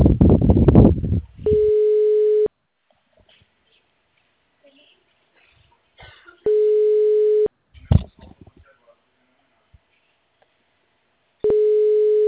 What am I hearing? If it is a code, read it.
Ringback tone (Europe, 425 Hz), with speech.